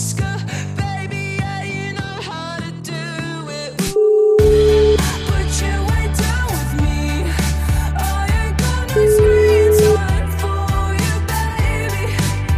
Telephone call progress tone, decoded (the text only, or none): ringback tone